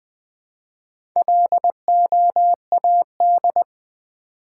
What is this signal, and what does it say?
Morse code, heard as a keyed tone, about 20 wpm, 700 Hz. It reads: LOAD
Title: Build A Word - 1 Letter At A Time - 4-Letter Words - 20wpm